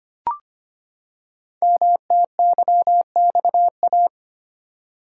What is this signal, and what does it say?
Morse code, heard as a keyed tone, about 25 wpm, 700 Hz. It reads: MTYXA
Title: Code Groups - 5 Letters - 25wpm